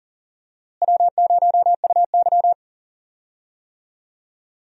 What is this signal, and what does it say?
Morse code, heard as a keyed tone, about 39 wpm, 700 Hz. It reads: W0UY
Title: USA Call Signs - 4 Characters - Repeated 3x - 40wpm